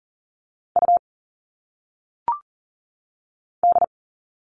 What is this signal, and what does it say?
Morse code, heard as a keyed tone, about 39 wpm, 700 Hz. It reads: U D
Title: All Cut Numbers - 1-Digit 40wpm